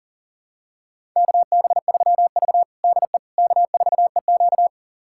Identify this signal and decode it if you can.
Morse code, heard as a keyed tone, about 40 wpm, 700 Hz. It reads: KB3V DE K4EQ
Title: USA Callsign DE Callsign - 4 Characters - 40wpm